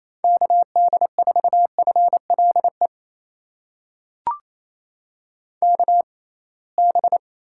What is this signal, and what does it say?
Morse code, heard as a keyed tone, about 28 wpm, spaced as for 21 wpm, 700 Hz. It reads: KD4FLE KB
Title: USA Call Signs - 6x Character Spacing - 28wpm